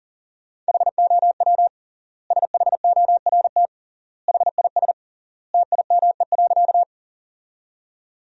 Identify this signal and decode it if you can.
Morse code, heard as a keyed tone, about 40 wpm, 700 Hz. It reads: HOW SHORT HIS TIME.
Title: 3x Word-Spacing - Sentences from Top 400 Words - 40wpm